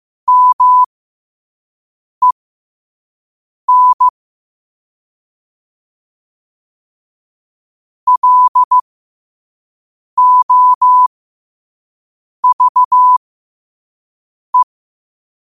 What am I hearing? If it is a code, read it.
Morse code, heard as a keyed tone, about 15 wpm, spaced as for 5 wpm, 1000 Hz. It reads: MEN LOVE
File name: qotdfiles / 5wpm